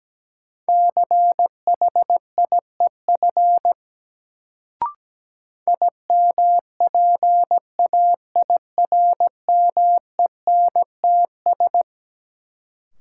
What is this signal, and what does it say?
Morse code, heard as a keyed tone, about 17 wpm, 700 Hz. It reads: CHIEF IMPAIRMENTS
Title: Sets of 5 Words Encore - New Word Drill - 17wpm